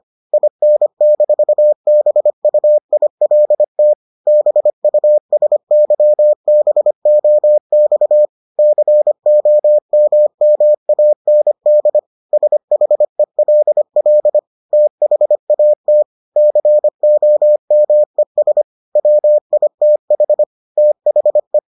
Morse code, heard as a keyed tone, about 25 wpm, 600 Hz. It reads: IN-BUILT BUSYBOX COMMAND SHELL THAT COMES WITH THE